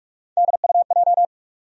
Morse code, heard as a keyed tone, about 46 wpm, 700 Hz. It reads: DUJ
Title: USA Call Sign Postfixes - 3 Letters 45wpm